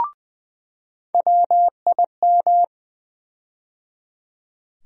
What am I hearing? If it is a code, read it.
Morse code, heard as a keyed tone, about 20 wpm, 700 Hz. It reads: WIM
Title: USA Call Sign Postfixes - 3 Letters 20wpm